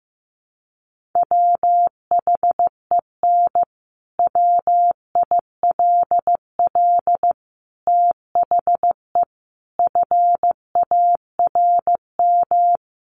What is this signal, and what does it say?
Morse code, heard as a keyed tone, about 15 wpm, 700 Hz. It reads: WHEN WILL THE FARM
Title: Sentences from Top 1400 Words 15wpm